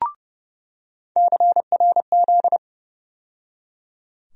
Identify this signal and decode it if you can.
Morse code, heard as a keyed tone, about 30 wpm, 700 Hz. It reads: CRZ
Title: USA Call Sign Postfixes - 3 Letters 30wpm